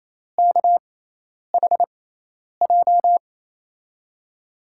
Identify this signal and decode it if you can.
Morse code, heard as a keyed tone, about 28 wpm, spaced as for 10 wpm, 700 Hz. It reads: KHJ
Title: USA Call Signs - 6 Characters - 6x Character Spacing - 28wpm